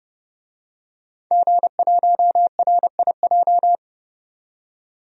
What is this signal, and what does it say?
Morse code, heard as a keyed tone, about 30 wpm, 700 Hz. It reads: G1RIJ
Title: DXCC-100 Call Signs - 5 Characters - 30wpm